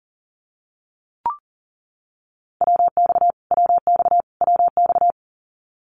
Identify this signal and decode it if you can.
Morse code, heard as a keyed tone, about 40 wpm, 700 Hz. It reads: WX WX WX
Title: Abbreviations Repeated 3x - 40wpm